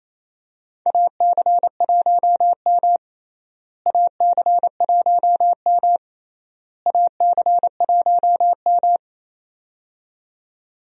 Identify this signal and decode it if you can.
Morse code, heard as a keyed tone, about 28 wpm, 700 Hz. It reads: AC1M AC1M AC1M
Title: USA Call Signs - 4 Characters - Repeated 3x - 28wpm